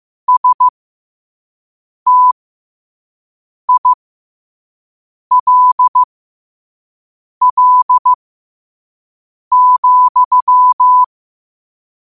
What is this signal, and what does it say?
Morse code, heard as a keyed tone, about 15 wpm, spaced as for 5 wpm, 1000 Hz. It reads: STILL,
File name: qotdfiles / 5wpm